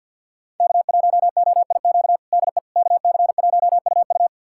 Morse code, heard as a keyed tone, about 52 wpm, 700 Hz. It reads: K1OIQ DE KC1UU
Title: USA Callsign DE Callsign - 5 Characters - 50wpm